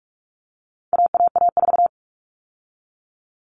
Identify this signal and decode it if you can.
Morse code, heard as a keyed tone, about 44 wpm, 700 Hz. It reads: AAA4
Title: All Cut Numbers - 4-Digit 45wpm